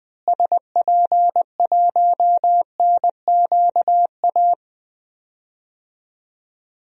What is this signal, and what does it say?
Morse code, heard as a keyed tone, about 20 wpm, 700 Hz. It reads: SP1NQA
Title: DXCC-100 Call Signs - 6 Characters - Repeated 3x - 20wpm